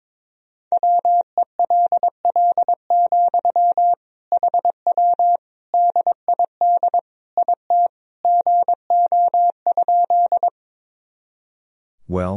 Morse code, heard as a keyed tone, about 22 wpm, 700 Hz. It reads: WELL, HW DID IT GO?
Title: Sentences from Top 200 Words and Abbreviations 22wpm